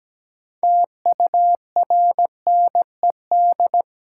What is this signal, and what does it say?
Morse code, heard as a keyed tone, about 17 wpm, 700 Hz. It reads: TURNED
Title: Sets of 4 Words - New Word Drill - 17wpm